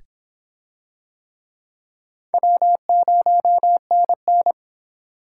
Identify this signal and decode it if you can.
Morse code, heard as a keyed tone, about 26 wpm, 700 Hz. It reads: W0NN